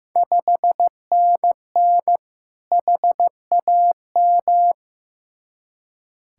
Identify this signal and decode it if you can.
Morse code, heard as a keyed tone, about 15 wpm, 700 Hz. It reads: 5NN HAM
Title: Ontario QSO Party Exchange 15wpm